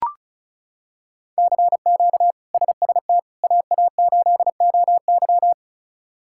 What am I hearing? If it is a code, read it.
Morse code, heard as a keyed tone, about 34 wpm, 700 Hz. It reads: CQ SST AA8OY
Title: K1USN SST QSO Practice - 35wpm